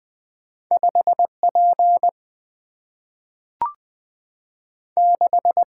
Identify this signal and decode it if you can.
Morse code, heard as a keyed tone, about 20 wpm, 700 Hz. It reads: 5P 6
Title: Single-Letter Single-Number 20wpm